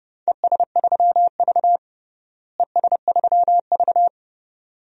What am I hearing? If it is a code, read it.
Morse code, heard as a keyed tone, about 30 wpm, 700 Hz. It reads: ES3V ES3V
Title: DXCC-100 Call Signs - 4 Characters - Repeated 2x - 30wpm